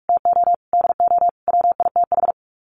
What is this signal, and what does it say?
Morse code, heard as a keyed tone, about 44 wpm, 700 Hz. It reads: TO DO WITH
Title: Sets of 5 Words 45wpm